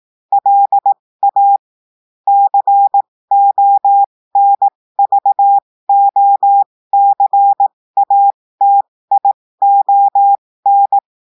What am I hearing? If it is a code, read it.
Morse code, heard as a keyed tone, about 18 wpm, spaced as for 15 wpm, 800 Hz. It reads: LA CONVOCATION
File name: cours-du-lundi files / cours-1580-texte-clair-15-mpm